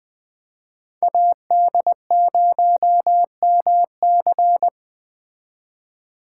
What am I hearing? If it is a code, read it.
Morse code, heard as a keyed tone, about 21 wpm, 700 Hz. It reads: AD0MC
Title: Speed Racing - USA 2x2 Call Signs 17wpm